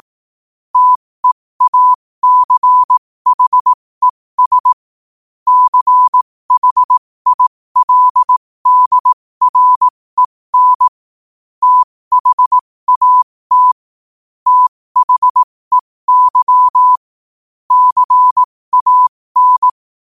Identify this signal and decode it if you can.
Morse code, heard as a keyed tone, about 18 wpm, spaced as for 15 wpm, 1000 Hz. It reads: TEACHES CHILDREN THAT THEY CAN
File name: qotdfiles / 15wpm